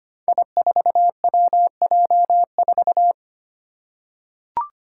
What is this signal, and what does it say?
Morse code, heard as a keyed tone, about 25 wpm, 700 Hz. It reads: I4WJ4
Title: Five Challenging Letter-Number Combinations 25wpm